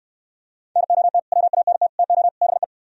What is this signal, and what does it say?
Morse code, heard as a keyed tone, about 67 wpm, 700 Hz. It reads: NOT WANT TO BE